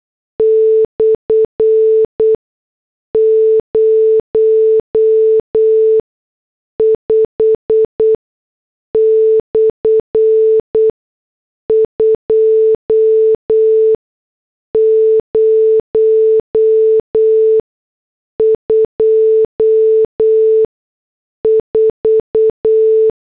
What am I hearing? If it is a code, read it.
Morse code, heard as a keyed tone, about 8 wpm, 440 Hz. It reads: / 0 5 / 2 0 2 4
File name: morse / headlines